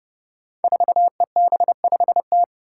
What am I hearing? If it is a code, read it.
Morse code, heard as a keyed tone, about 32 wpm, 700 Hz. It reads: 4EB5T
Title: Five Challenging Letter-Number Combinations 30wpm